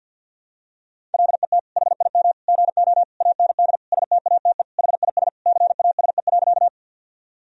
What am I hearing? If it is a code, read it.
Morse code, heard as a keyed tone, about 52 wpm, 700 Hz. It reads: LET HIM GO AND STATE HIS CASE.
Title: Sentences from Top 1000 Words - Review Part 2 - 50wpm